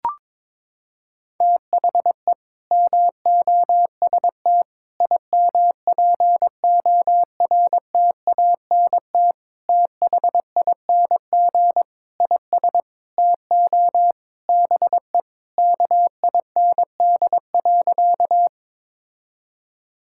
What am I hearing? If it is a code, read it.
Morse code, heard as a keyed tone, about 22 wpm, 700 Hz. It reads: THE MOST IMPORTANT THING IS TO BE KIND.
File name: practice / 22wpm / Sentences with 4-Word Phrases - 22wpm